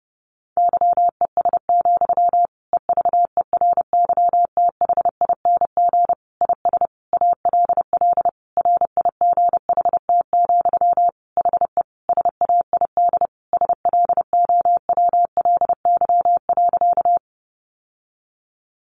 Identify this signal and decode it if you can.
Morse code, heard as a keyed tone, about 30 wpm, 700 Hz. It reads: YES, EVERYTHING IS ALL RIGHT, HE SAID SLOWLY.
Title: Sentences from Top 1100 Words 30wpm